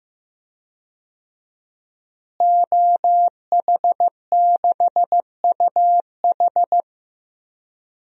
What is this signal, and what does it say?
Morse code, heard as a keyed tone, about 15 wpm, 700 Hz. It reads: OH6UH
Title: DXCC-100 Call Signs - 5 Characters - Repeated 2x - 15wpm